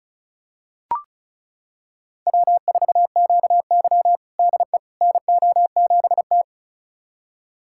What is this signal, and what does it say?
Morse code, heard as a keyed tone, about 36 wpm, 700 Hz. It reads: W4QY DE NO7T
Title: USA Callsign DE Callsign - 4 Characters - 35wpm